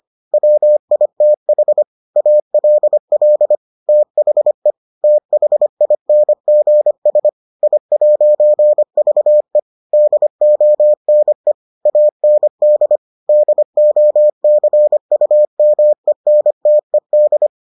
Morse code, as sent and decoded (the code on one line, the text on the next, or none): .-- .. - .... / .- .-.. .-.. / - .... / - .... .. -. --. ... / .. .----. ...- . / -.. --- -. / .- -. -.. / -.. --- -.-. ..- -- . -. - . -..
WITH ALL THE THINGS I'VE DONE AND DOCUMENTED